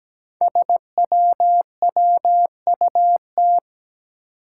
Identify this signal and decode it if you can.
Morse code, heard as a keyed tone, about 17 wpm, 700 Hz. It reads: SWWUT
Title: Five Challenging Letter-Number Combinations 17wpm